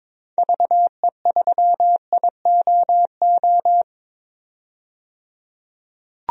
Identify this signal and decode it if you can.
Morse code, heard as a keyed tone, about 22 wpm, 700 Hz. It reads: VE3IOO
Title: DXCC-100 Call Signs - 6 Characters - 22wpm